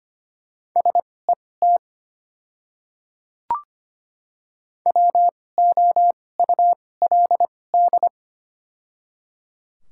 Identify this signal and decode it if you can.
Morse code, heard as a keyed tone, about 25 wpm, spaced as for 18 wpm, 700 Hz. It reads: SET WOULD
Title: 2x Character Spacing - Top 300 Words - 25wpm